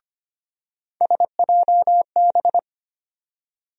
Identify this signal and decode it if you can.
Morse code, heard as a keyed tone, about 25 wpm, 700 Hz. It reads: SJB
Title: USA Call Sign Postfixes - 3 Letters 25wpm